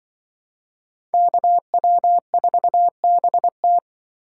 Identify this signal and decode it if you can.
Morse code, heard as a keyed tone, about 28 wpm, 700 Hz. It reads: KW4BT